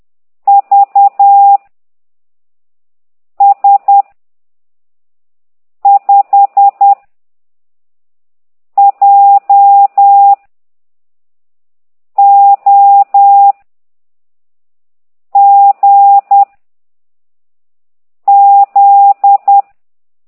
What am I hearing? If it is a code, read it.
Morse code, heard as a keyed tone, about 10 wpm, 800 Hz. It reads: V S 5 J O G Z
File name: ct2idw / exerc13